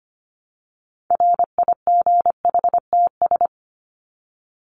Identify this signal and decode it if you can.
Morse code, heard as a keyed tone, about 25 wpm, 700 Hz. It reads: RIGHTS
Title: ES:S Suffix - Top 100 Words with Base Words 25wpm